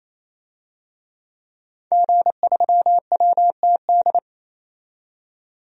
Morse code, heard as a keyed tone, about 28 wpm, 700 Hz. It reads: G3WTD
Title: DXCC-100 Call Signs - 5 Characters - 28wpm